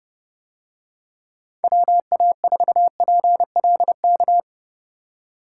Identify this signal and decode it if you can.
Morse code, heard as a keyed tone, about 30 wpm, 700 Hz. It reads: WA4PLK